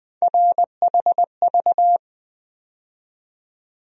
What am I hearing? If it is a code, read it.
Morse code, heard as a keyed tone, about 20 wpm, 700 Hz. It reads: RHV
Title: USA Call Sign Postfixes - 3 Letters 20wpm